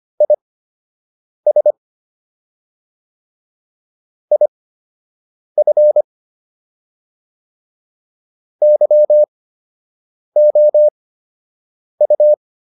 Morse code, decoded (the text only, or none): IS IF YOU